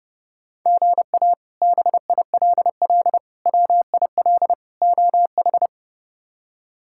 Morse code, heard as a keyed tone, about 32 wpm, 700 Hz. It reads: GA BILL WIL OH
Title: K1USN SST QSO Practice - 30wpm